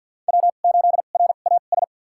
Morse code, heard as a keyed tone, about 52 wpm, spaced as for 37 wpm, 690 Hz. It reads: W8RAS
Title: USA Call Signs - 2x Character Spacing - 50wpm